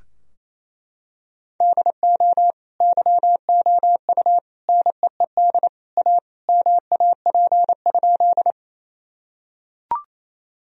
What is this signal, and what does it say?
Morse code, heard as a keyed tone, about 28 wpm, 700 Hz. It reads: DO YOU NEED A MAP?
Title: Sentences from Top 1000 Words - Review Part 2 - 28wpm